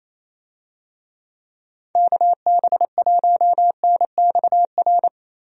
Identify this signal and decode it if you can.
Morse code, heard as a keyed tone, about 28 wpm, 700 Hz. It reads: KB1NXR
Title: USA Call Signs - 6 Characters - Repeated 3x - 28wpm